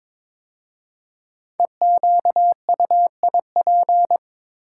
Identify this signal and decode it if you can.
Morse code, heard as a keyed tone, about 22 wpm, 690 Hz. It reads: EQUIP
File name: 5 Letter Words - First 3 Letters Given - 22wpm